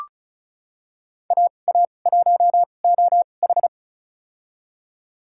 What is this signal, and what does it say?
Morse code, heard as a keyed tone, about 36 wpm, spaced as for 25 wpm, 700 Hz. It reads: AA1OH
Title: USA Call Signs - 2x Character Spacing - 35wpm